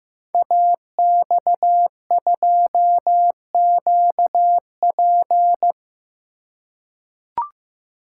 Morse code, heard as a keyed tone, about 15 wpm, 700 Hz. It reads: AX2QP
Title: Code Groups - 5 Letter:Numbers - 15wpm